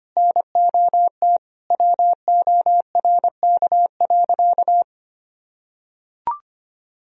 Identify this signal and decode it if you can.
Morse code, heard as a keyed tone, about 25 wpm, 700 Hz. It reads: NOT WORK.
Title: Sentences from Top 1000 Words - Review Part 3 - 25wpm